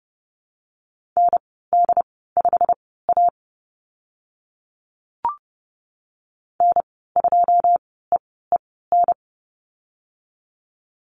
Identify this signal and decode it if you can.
Morse code, heard as a keyed tone, about 31 wpm, spaced as for 17 wpm, 700 Hz. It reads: ND5A N2EEN
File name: Lesson 19g - Letter w - Call Signs 30wpm